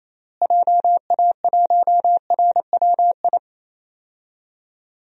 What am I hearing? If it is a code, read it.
Morse code, heard as a keyed tone, about 28 wpm, 700 Hz. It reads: JA1RWI
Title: USA Call Signs - 6 Characters - Repeated 2x - 28wpm